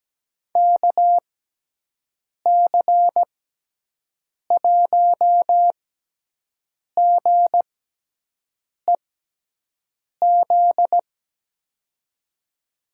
Morse code, heard as a keyed tone, about 17 wpm, spaced as for 6 wpm, 700 Hz. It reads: KC1GEZ
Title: USA Call Signs - 6 Characters - 6x Character Spacing - 17wpm